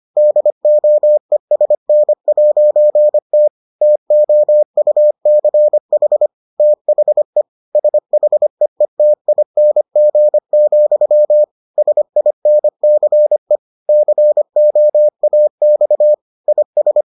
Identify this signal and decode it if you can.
Morse code, heard as a keyed tone, about 25 wpm, 600 Hz. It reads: DOESN'T TOUCH THE SHEETING, SINCE COAX IS